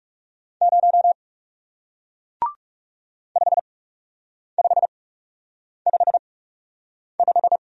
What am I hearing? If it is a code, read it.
Morse code, heard as a keyed tone, about 39 wpm, spaced as for 8 wpm, 700 Hz. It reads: 0 5555